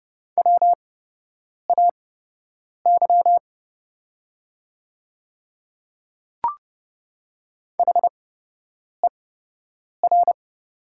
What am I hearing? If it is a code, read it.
Morse code, heard as a keyed tone, about 30 wpm, spaced as for 8 wpm, 700 Hz. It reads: WAY HER
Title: 8x Character Spacing - Top 100 Words - 30wpm